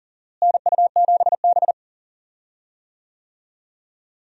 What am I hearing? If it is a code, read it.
Morse code, heard as a keyed tone, about 40 wpm, 700 Hz. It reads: NU7B
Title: USA Call Signs - 4 Characters - Repeated 3x - 40wpm